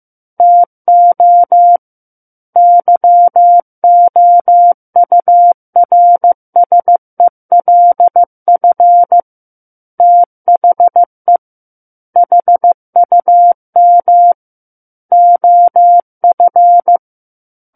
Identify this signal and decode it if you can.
Morse code, heard as a keyed tone, about 15 wpm, 700 Hz. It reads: TO YOURSELF THE HUM OF
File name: Chapter 6 - 15 WPM